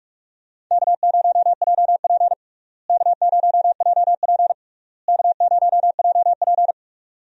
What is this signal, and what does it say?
Morse code, heard as a keyed tone, about 46 wpm, 700 Hz. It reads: K0JP K0JP K0JP
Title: USA Call Signs - 4 Characters - Repeated 3x - 45wpm